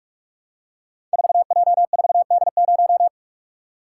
Morse code, heard as a keyed tone, about 44 wpm, 700 Hz. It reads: 4J4D0
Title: Code Groups - 5 Letter:Numbers - 45wpm